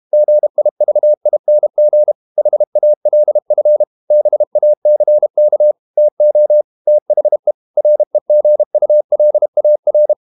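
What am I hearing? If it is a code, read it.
Morse code, heard as a keyed tone, about 32 wpm, 600 Hz. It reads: GIVING HALF BACK TO THE REGULAR
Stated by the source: Morse code practice at 32 wpm from CW med Gnister, reading Ken Follett's novel Never in English.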